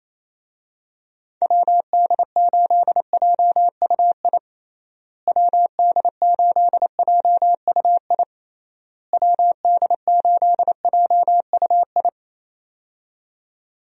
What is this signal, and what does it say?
Morse code, heard as a keyed tone, about 28 wpm, 700 Hz. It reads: WD8JUI WD8JUI WD8JUI
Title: USA Call Signs - 6 Characters - Repeated 3x - 28wpm